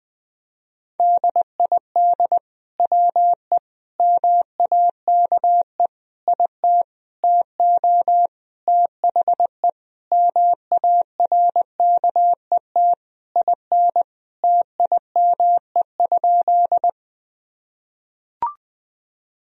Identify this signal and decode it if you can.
Morse code, heard as a keyed tone, about 20 wpm, 700 Hz. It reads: DID WE MAKE IT TO THE MARKET IN TIME?
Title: Sentences from Top 1000 Words - Review Part 3 - 20wpm